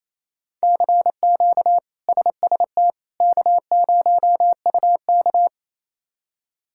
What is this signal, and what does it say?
Morse code, heard as a keyed tone, about 28 wpm, 700 Hz. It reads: CQ SST K0UK
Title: K1USN SST QSO Practice - 28wpm